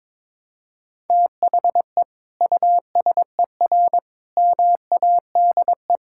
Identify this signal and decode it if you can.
Morse code, heard as a keyed tone, about 22 wpm, 700 Hz. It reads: THE USER MADE
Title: Sentences from Top 1000 Words - Review Part 5 - 22wpm